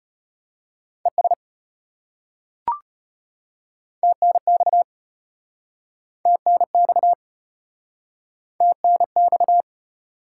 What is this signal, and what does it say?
Morse code, heard as a keyed tone, about 34 wpm, 700 Hz. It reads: ES TNX TNX TNX